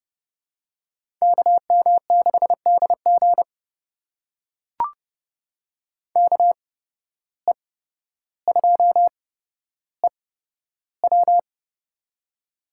Morse code, heard as a keyed tone, about 30 wpm, spaced as for 13 wpm, 700 Hz. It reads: KM6DG KE2EW
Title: USA Call Signs - 5 Characters - 8x Character Spacing - 30wpm